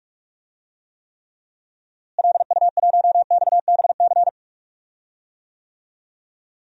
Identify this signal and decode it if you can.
Morse code, heard as a keyed tone, about 46 wpm, 700 Hz. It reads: LU1XBC
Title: DXCC-100 Call Signs - 6 Characters - Repeated 3x - 45wpm